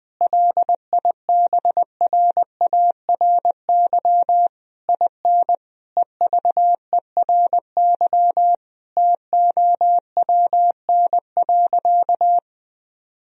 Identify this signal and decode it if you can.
Morse code, heard as a keyed tone, about 21 wpm, 700 Hz. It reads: LIBRARY IN EVERY TOWN.
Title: Sentences from Top 1000 Words - Review Part 4 - 20wpm